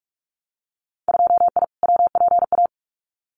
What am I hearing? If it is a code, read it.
Morse code, heard as a keyed tone, about 44 wpm, 700 Hz. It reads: 2I WPA
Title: Winter Field Day Exchanges 45wpm